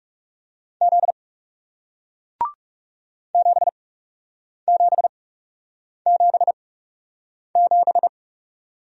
Morse code, heard as a keyed tone, about 40 wpm, spaced as for 8 wpm, 700 Hz. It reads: Z 7777